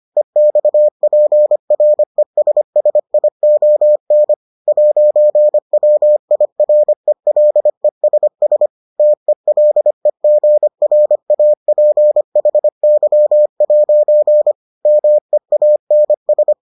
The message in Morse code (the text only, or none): EXPRESSION 'WIRELESS TELEGRAPHY' MEANS